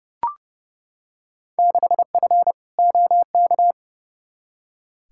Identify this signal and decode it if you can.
Morse code, heard as a keyed tone, about 30 wpm, 700 Hz. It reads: 6F OK
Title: Field Day Exchanges Encore - 30wpm